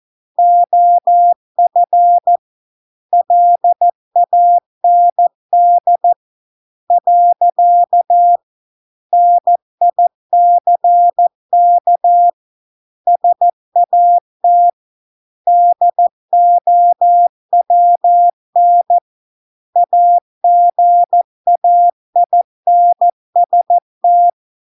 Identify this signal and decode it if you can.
Morse code, heard as a keyed tone, about 14 wpm, 700 Hz. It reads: OF LAND . NICK SAT DOWN AGAINST